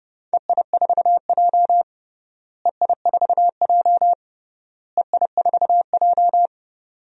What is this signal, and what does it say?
Morse code, heard as a keyed tone, about 30 wpm, 700 Hz. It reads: EI4J EI4J EI4J
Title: DXCC-100 Call Signs - 4 Characters - Repeated 3x - 30wpm